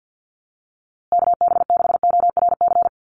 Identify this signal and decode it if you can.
Morse code, heard as a keyed tone, about 50 wpm, 700 Hz. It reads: KB6ORC
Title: USA Call Signs Repeated 2x 50wpm